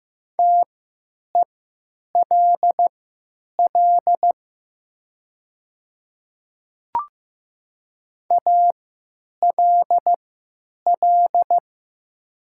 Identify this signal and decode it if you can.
Morse code, heard as a keyed tone, about 15 wpm, spaced as for 9 wpm, 700 Hz. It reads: TELL ALL
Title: 3x Character Spacing - Top 500 Words - 15wpm